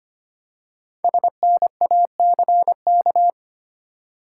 Morse code, heard as a keyed tone, about 25 wpm, 700 Hz. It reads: SNACK
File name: practice / 25wpm / Easy Words with Hints - 25wpm